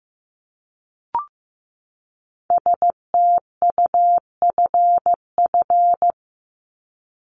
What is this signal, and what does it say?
Morse code, heard as a keyed tone, about 15 wpm, 700 Hz. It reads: STUFF
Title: Top 1251-1500 Words 15wpm